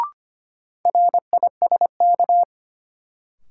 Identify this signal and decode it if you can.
Morse code, heard as a keyed tone, about 25 wpm, 700 Hz. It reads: RISK
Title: Sentences from Top 600 Words - New Word Drill - 25wpm